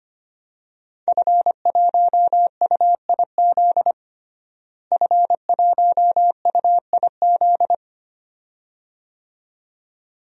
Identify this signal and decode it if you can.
Morse code, heard as a keyed tone, about 25 wpm, 700 Hz. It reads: F1UIZ F1UIZ
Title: DXCC-100 Call Signs - 5 Characters - Repeated 2x - 25wpm